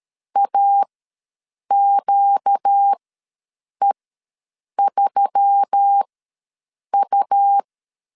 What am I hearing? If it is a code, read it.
Morse code, heard as a keyed tone, about 13 wpm, spaced as for 7 wpm, 800 Hz. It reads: AQE3U
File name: LEZIONI / 7 GF9Z / 7 GF9Z ws 12wpm 3321